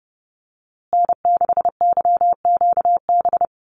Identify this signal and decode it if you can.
Morse code, heard as a keyed tone, about 30 wpm, 700 Hz. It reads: N6YQB
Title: USA Call Signs Repeated 2x 30wpm